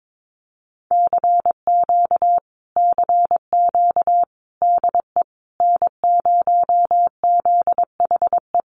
Morse code, heard as a keyed tone, about 22 wpm, 700 Hz. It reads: CQ CQ DE N0ZHE
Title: Common QSO Phrases Encore 22wpm